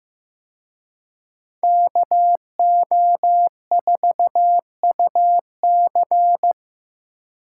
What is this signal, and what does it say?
Morse code, heard as a keyed tone, about 18 wpm, 700 Hz. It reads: KO4UC
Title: Speed Racing - USA 2x2 Call Signs 15wpm